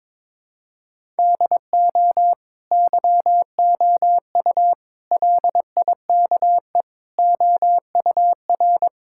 Morse code, heard as a keyed tone, about 21 wpm, 700 Hz. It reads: DO YOU LIKE OUR
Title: Sentences from Top 1000 Words - Review Part 6 - 22wpm